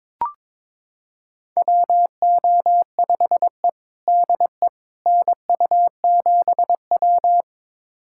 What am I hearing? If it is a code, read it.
Morse code, heard as a keyed tone, about 22 wpm, 700 Hz. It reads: WO5E DE NU7W
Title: USA Callsign DE Callsign - 4 Characters - 22wpm